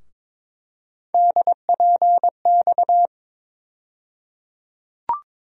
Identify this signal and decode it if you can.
Morse code, heard as a keyed tone, about 22 wpm, 700 Hz. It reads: DPX
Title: USA Call Sign Postfixes - 3 Letters 22wpm